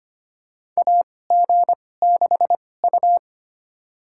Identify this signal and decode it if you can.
Morse code, heard as a keyed tone, about 25 wpm, spaced as for 18 wpm, 700 Hz. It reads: AG6U
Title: USA Call Signs - 4 Characters - 2x Character Spacing - 25wpm